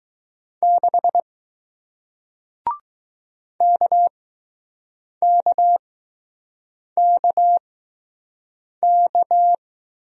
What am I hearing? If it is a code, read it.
Morse code, heard as a keyed tone, about 20 wpm, spaced as for 6 wpm, 700 Hz. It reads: 6 KKKK